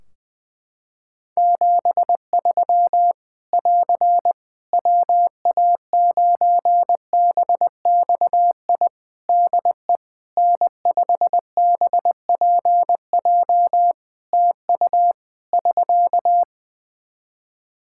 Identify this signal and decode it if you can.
Morse code, heard as a keyed tone, about 21 wpm, 700 Hz. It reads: 73 <AR> WA9BXI DE N5BPJ TU <SK>
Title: K1USN SST QSO Practice 20wpm